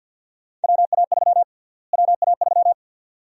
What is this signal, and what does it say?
Morse code, heard as a keyed tone, about 52 wpm, 700 Hz. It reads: WA3 WA3
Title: USA Call Sign Prefixes - Repeated 2x - 50wpm